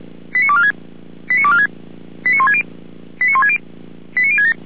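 Five-tone selective call (ZVEI), heard as quick runs of five tones, tones of about 70 ms. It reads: 89246 89246 89160 89160 89067
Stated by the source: ZVEI five-tone selective call received by SDR.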